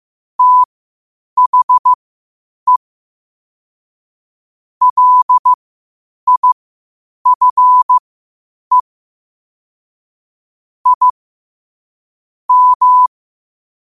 Morse code, heard as a keyed tone, about 15 wpm, spaced as for 8 wpm, 1000 Hz. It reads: THE LIFE I M
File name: qotdfiles / 8wpm